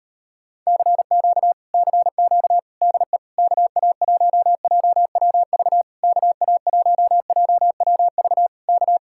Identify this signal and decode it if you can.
Morse code, heard as a keyed tone, about 33 wpm, 700 Hz. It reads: CQ CQ DE KA1JWV KA1JWV K